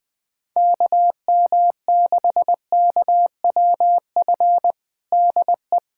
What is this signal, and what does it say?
Morse code, heard as a keyed tone, about 21 wpm, 700 Hz. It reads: KM6KWF DE
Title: USA Callsign DE Callsign - 6 Characters - 20wpm